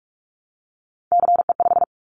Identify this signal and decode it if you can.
Morse code, heard as a keyed tone, about 44 wpm, 700 Hz. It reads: CE5
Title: Callsign Prefixes for Top 100 DXCC - 45wpm